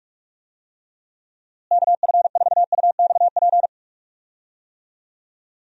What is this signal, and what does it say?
Morse code, heard as a keyed tone, about 46 wpm, 700 Hz. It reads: KF4UXP
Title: USA Call Signs - 6 Characters - Repeated 3x - 45wpm